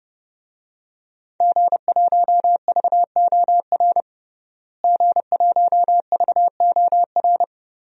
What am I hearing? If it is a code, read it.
Morse code, heard as a keyed tone, about 30 wpm, 700 Hz. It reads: G1VOR G1VOR
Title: DXCC-100 Call Signs - 5 Characters - Repeated 2x - 30wpm